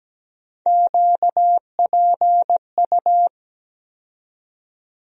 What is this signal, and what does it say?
Morse code, heard as a keyed tone, about 17 wpm, 700 Hz. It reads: QPU
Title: USA Call Sign Postfixes - 3 Letters 17wpm